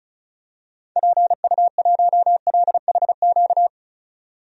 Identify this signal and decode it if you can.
Morse code, heard as a keyed tone, about 36 wpm, 700 Hz. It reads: PU1LHQ
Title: DXCC-100 Call Signs - 6 Characters - 35wpm